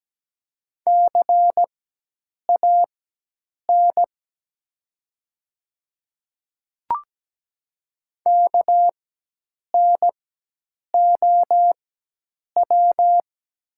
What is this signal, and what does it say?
Morse code, heard as a keyed tone, about 17 wpm, spaced as for 8 wpm, 700 Hz. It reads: CAN KNOW
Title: 4x Character Spacing - Top 100 Words - 17wpm